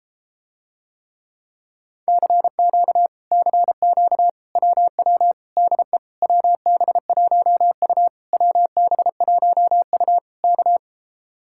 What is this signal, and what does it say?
Morse code, heard as a keyed tone, about 33 wpm, 700 Hz. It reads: CQ CQ WW DE WB1U WB1U K